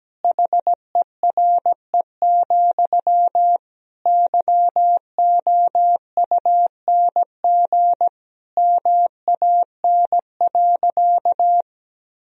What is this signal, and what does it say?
Morse code, heard as a keyed tone, about 17 wpm, 700 Hz. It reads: HERE, YOUNG MAN.
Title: Sentences from Top 1000 Words - Review Part 1 - 17wpm